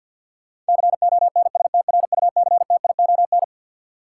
Morse code, heard as a keyed tone, about 52 wpm, 700 Hz. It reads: CONSTRUCTION
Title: Sentences from Top 1600 Words - New Word Drill - 50wpm